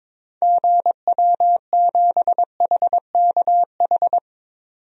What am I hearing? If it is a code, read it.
Morse code, heard as a keyed tone, about 22 wpm, 700 Hz. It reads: GW7HKH
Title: DXCC-100 Call Signs - 6 Characters - 22wpm